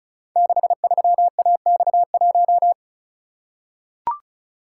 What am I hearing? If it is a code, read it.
Morse code, heard as a keyed tone, about 36 wpm, 700 Hz. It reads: 63AX1
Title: Code Groups - 5 Letter:Numbers - 35wpm